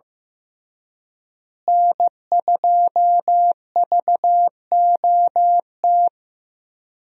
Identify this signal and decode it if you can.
Morse code, heard as a keyed tone, about 15 wpm, 700 Hz. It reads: N2VOT
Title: USA Call Signs - 5 Characters - Repeated 3x - 15wpm